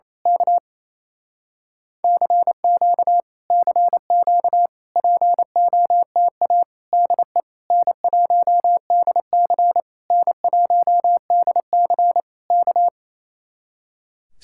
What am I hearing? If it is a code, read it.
Morse code, heard as a keyed tone, about 27 wpm, 700 Hz. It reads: K CQ CQ POTA DE N1DC N1DC K
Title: Speed-Racing Common QSO Phrases Encore - 28wpm